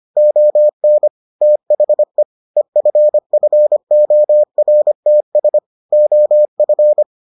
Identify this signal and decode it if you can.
Morse code, heard as a keyed tone, about 25 wpm, 600 Hz. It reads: ON THE EFFORTS OF